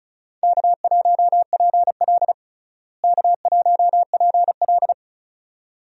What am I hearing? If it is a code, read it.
Morse code, heard as a keyed tone, about 36 wpm, 700 Hz. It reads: K1PL K1PL
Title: USA Call Signs - 4 Characters - Repeated 2x - 35wpm